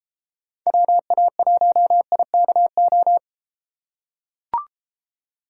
Morse code, heard as a keyed tone, about 33 wpm, 700 Hz. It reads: WA1IKO